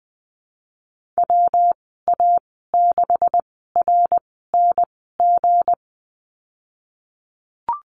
Morse code, heard as a keyed tone, about 20 wpm, spaced as for 14 wpm, 700 Hz. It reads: WA6RNG
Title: 179 Lesson 29g - Slash - Call Signs 20wpm